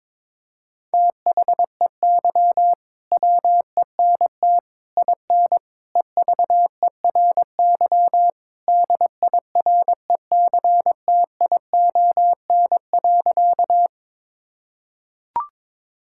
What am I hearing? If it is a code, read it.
Morse code, heard as a keyed tone, about 22 wpm, 700 Hz. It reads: THEY WENT IN EVERY DIRECTION.
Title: Sentences from Top 1000 Words - Review Part 5 - 22wpm